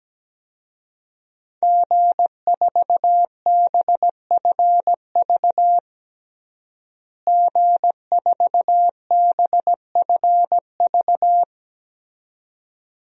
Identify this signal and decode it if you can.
Morse code, heard as a keyed tone, about 17 wpm, 700 Hz. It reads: G4BFV G4BFV
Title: DXCC-100 Call Signs - 5 Characters - Repeated 2x - 17wpm